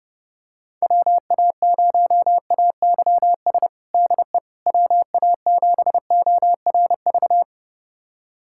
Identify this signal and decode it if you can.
Morse code, heard as a keyed tone, about 30 wpm, 690 Hz. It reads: WA0AYS DE WA7ORV
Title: USA Callsign DE Callsign - 6 Characters - 30wpm